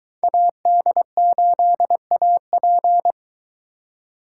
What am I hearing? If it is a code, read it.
Morse code, heard as a keyed tone, about 20 wpm, 700 Hz. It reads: AD8AP